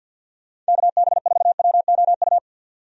Morse code, heard as a keyed tone, about 50 wpm, 700 Hz. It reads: KB4WOU
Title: USA Call Signs - 6 Characters - 50wpm